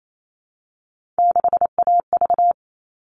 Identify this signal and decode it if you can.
Morse code, heard as a keyed tone, about 28 wpm, 700 Hz. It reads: 6AV
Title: All Cut Numbers - 3-Digit 28wpm